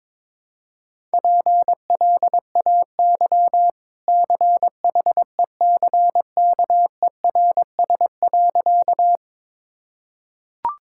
Morse code, heard as a keyed tone, about 22 wpm, 690 Hz. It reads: PLAY CHECKERS.
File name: practice / 22wpm / Easy Sentences with Hints - 22wpm